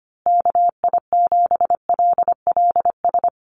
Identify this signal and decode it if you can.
Morse code, heard as a keyed tone, about 25 wpm, 700 Hz. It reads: KI7LLS
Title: USA Call Signs Repeated 3x 25wpm